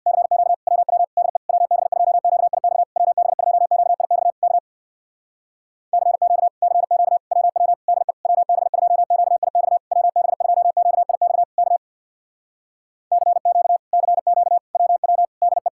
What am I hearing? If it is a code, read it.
Morse code, heard as a keyed tone, about 64 wpm, 700 Hz. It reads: CQ WW DE WB2QIX WB2QIX K CQ CQ WW DE WB2QIX WB2QIX K CQ CQ WW DE